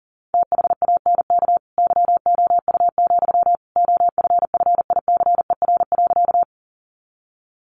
Morse code, heard as a keyed tone, about 40 wpm, 700 Hz. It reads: THANK YOU, OFFICER.
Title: Sentences from Top 1200 Words 40wpm